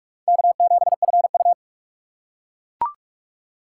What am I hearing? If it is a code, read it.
Morse code, heard as a keyed tone, about 45 wpm, 700 Hz. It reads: K7FU
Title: USA Call Signs - 2x Character Spacing - 45wpm